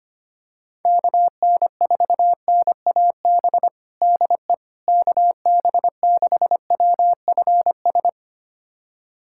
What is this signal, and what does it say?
Morse code, heard as a keyed tone, about 25 wpm, 700 Hz. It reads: KN4NAB DE KB6WFS
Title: USA Callsign DE Callsign - 6 Characters - 25wpm